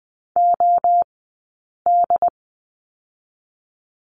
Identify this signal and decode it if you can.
Morse code, heard as a keyed tone, about 19 wpm, spaced as for 8 wpm, 700 Hz. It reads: OD
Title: 080 Lesson 14d - Number 2 - Two Characters 20wpm